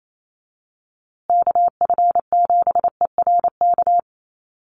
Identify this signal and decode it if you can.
Morse code, heard as a keyed tone, about 28 wpm, 700 Hz. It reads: KF7ERK
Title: USA Call Signs Repeated 3x 28wpm